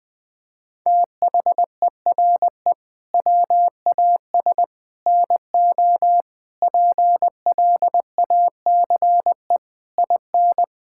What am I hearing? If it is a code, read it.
Morse code, heard as a keyed tone, about 19 wpm, 700 Hz. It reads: THERE WAS NO PLACE IN
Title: Sentences from Top 1600 Words 20wpm